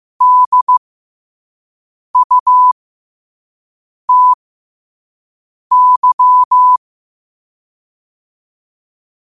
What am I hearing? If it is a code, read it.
Morse code, heard as a keyed tone, about 15 wpm, spaced as for 5 wpm, 1000 Hz. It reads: DUTY
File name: qotdfiles / 5wpm